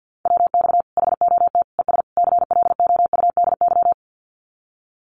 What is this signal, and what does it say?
Morse code, heard as a keyed tone, about 50 wpm, 700 Hz. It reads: WX HOT ES CLOUDY
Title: Common QSO Phrases 50wpm